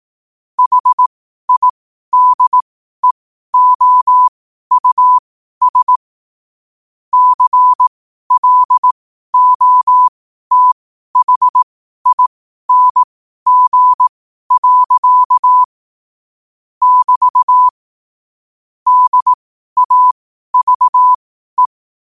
Morse code, heard as a keyed tone, about 18 wpm, spaced as for 12 wpm, 1000 Hz. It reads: HIDEOUS CLOTHING. <BT> DAVE
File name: qotdfiles / 12wpm